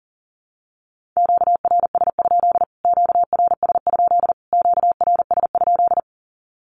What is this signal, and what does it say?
Morse code, heard as a keyed tone, about 40 wpm, 700 Hz. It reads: QRS? QRS? QRS?
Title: Q-Codes Repeated 3x 40wpm